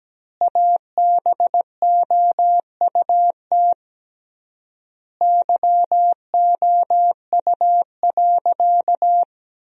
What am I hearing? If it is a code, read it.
Morse code, heard as a keyed tone, about 17 wpm, 700 Hz. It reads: ABOUT YOU.
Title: 3x Word-Spacing - Sentences from Top 400 Words - 17wpm